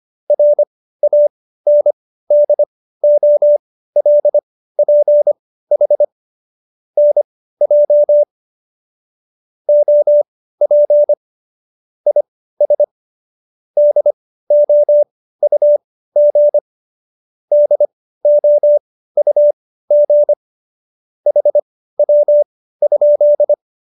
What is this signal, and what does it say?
Morse code, heard as a keyed tone, about 25 wpm, spaced as for 15 wpm, 590 Hz. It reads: RANDOLPH NJ OP IS DOUG DOUG HW?
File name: qso220b 15